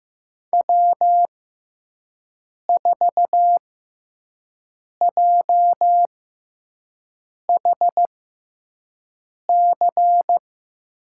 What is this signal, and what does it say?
Morse code, heard as a keyed tone, about 15 wpm, spaced as for 5 wpm, 700 Hz. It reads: W4JHC
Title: USA Call Signs - 6x Character Spacing - 15wpm